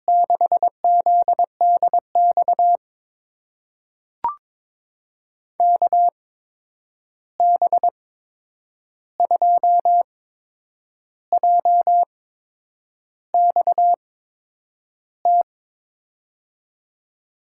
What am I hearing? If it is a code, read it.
Morse code, heard as a keyed tone, about 22 wpm, spaced as for 8 wpm, 700 Hz. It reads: 6ZDX KB2JXT
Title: USA Call Signs - 8x Character Spacing - 22wpm